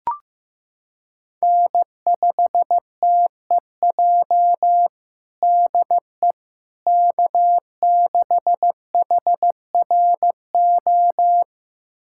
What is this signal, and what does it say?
Morse code, heard as a keyed tone, about 16 wpm, 700 Hz. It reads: N5TEJ DE K6HRO
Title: USA Callsign DE Callsign 15wpm